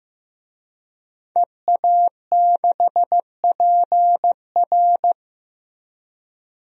Morse code, heard as a keyed tone, about 16 wpm, 700 Hz. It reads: EA6PR
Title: DXCC-100 Call Signs - 5 Characters - Repeated 3x - 15wpm